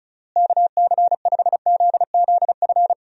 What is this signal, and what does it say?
Morse code, heard as a keyed tone, about 34 wpm, 700 Hz. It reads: KC5ZZF
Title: USA Call Signs - 6 Characters - Repeated 3x - 35wpm